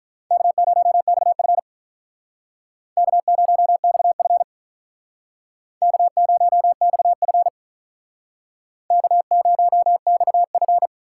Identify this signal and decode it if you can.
Morse code, heard as a keyed tone, about 42 wpm, 700 Hz. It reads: K0XF K0XF K0XF K0XF
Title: Speed Racing - USA 1x2 Call Signs 35wpm